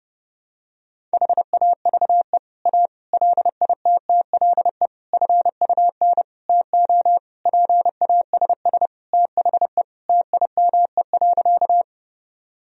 Morse code, heard as a keyed tone, about 28 wpm, 700 Hz. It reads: HAVE A LITTLE FUN TO PASS THE TIME.
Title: Sentences from Top 1000 Words - Review Part 4 - 30wpm